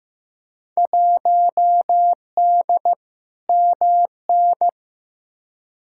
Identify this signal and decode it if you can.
Morse code, heard as a keyed tone, about 15 wpm, 700 Hz. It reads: 1D MN
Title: Field Day Exchanges - 15wpm